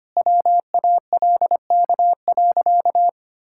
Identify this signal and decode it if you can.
Morse code, heard as a keyed tone, about 25 wpm, 700 Hz. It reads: WALK.
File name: practice / 25wpm / Sentences with 2-Word Phrases - 25wpm